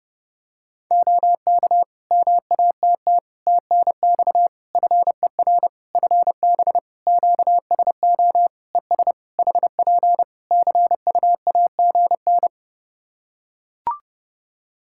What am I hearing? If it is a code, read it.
Morse code, heard as a keyed tone, about 30 wpm, 700 Hz. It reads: OK MATT TNX FER FB QSO ES HP CUAGN
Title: Practice QSO - Exchange 4 - Ending - 30wpm